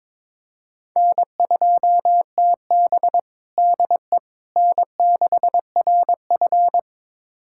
Morse code, heard as a keyed tone, about 22 wpm, 700 Hz. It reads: N2TB DE N6RF
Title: USA Callsign DE Callsign - 4 Characters - 22wpm